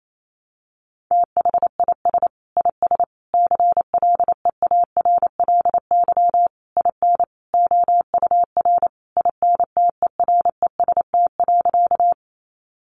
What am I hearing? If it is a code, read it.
Morse code, heard as a keyed tone, about 28 wpm, 700 Hz. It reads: THIS IS CLEARLY IN OUR INTEREST.
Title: Sentences from Top 1300 Words 28wpm